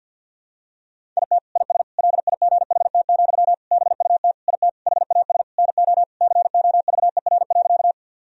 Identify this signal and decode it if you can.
Morse code, heard as a keyed tone, about 51 wpm, 700 Hz. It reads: IT IS LIGHT, BUT IT HAS NO COVER.
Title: Sentences from Top 1000 Words - Review Part 1 - 50wpm